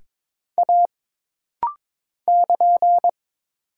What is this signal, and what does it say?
Morse code, heard as a keyed tone, about 22 wpm, 700 Hz. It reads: A <KN>
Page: This practice set is composed of Letters, Numbers, common punctuations, and common prosigns.